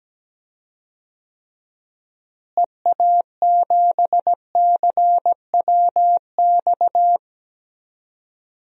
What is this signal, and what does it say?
Morse code, heard as a keyed tone, about 17 wpm, 700 Hz. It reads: EA7CWX
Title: DXCC-100 Call Signs - 6 Characters - Repeated 2x - 17wpm